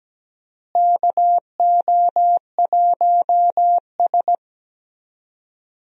Morse code, heard as a keyed tone, about 17 wpm, 700 Hz. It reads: KO1S
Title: USA Call Signs - 4 Characters - Repeated 3x - 17wpm